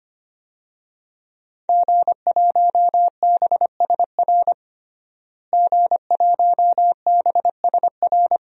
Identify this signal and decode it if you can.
Morse code, heard as a keyed tone, about 25 wpm, 700 Hz. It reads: G1BSR G1BSR
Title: DXCC-100 Call Signs - 5 Characters - Repeated 3x - 25wpm